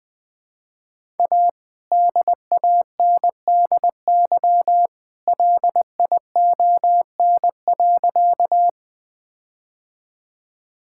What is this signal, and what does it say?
Morse code, heard as a keyed tone, about 20 wpm, 700 Hz. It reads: A DANDY LION.
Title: All-Age Appropriate Jokes - 20wpm